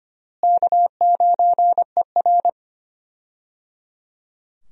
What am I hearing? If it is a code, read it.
Morse code, heard as a keyed tone, about 25 wpm, 700 Hz. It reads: K9ER
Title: USA Call Signs - 4 Characters - Repeated 3x - 25wpm